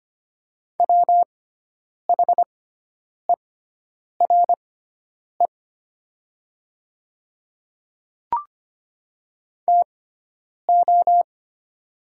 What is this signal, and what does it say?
Morse code, heard as a keyed tone, about 25 wpm, spaced as for 9 wpm, 700 Hz. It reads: WHERE TO
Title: 6x Character Spacing - Top 200 Words - 25wpm